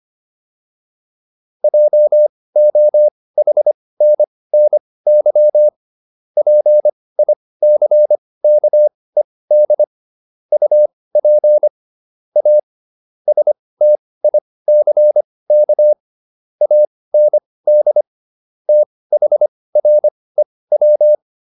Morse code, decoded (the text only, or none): JOHNNY PICKED UP A STICK AND THREW